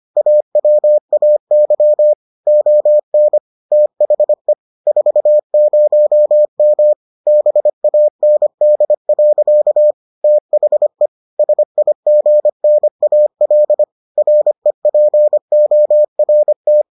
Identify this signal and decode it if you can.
Morse code, heard as a keyed tone, about 25 wpm, 600 Hz. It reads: AWAY ON THE 40M BAND. THE SIGNAL REPORT